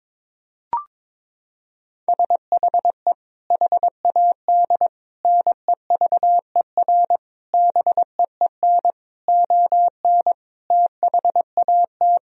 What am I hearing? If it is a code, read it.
Morse code, heard as a keyed tone, about 22 wpm, 700 Hz. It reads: SHE HAD NEVER BEEN ON THAT
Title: Sentences from Top 1000 Words - Review Part 2 - 22wpm